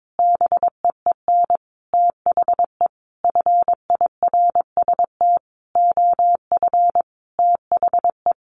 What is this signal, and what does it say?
Morse code, heard as a keyed tone, about 22 wpm, 700 Hz. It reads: BEEN THE FIRST OF THE
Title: Sentences from Top 800 Words 22wpm